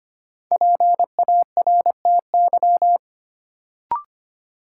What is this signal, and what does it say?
Morse code, heard as a keyed tone, about 25 wpm, 700 Hz. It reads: PARTY
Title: Sentences from Top 1400 Words - New Word Drill Alt - 25wpm